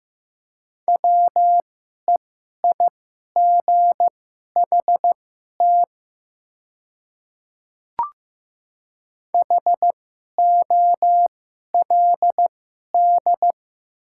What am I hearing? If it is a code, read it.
Morse code, heard as a keyed tone, about 15 wpm, spaced as for 11 wpm, 700 Hz. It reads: WEIGHT HOLD
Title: 2x Character Spacing - Top 500 Words - 15wpm